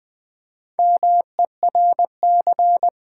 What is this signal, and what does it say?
Morse code, heard as a keyed tone, about 20 wpm, 700 Hz. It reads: MERC
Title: Build A Word - 2 Letters At A Time - 8-Letter Words - 20wpm